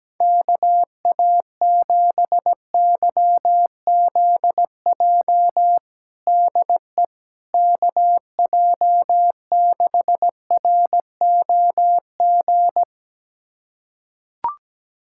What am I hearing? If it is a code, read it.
Morse code, heard as a keyed tone, about 17 wpm, 700 Hz. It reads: KA7YZJ DE KJ6ROG
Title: USA Callsign DE Callsign - 6 Characters - 17wpm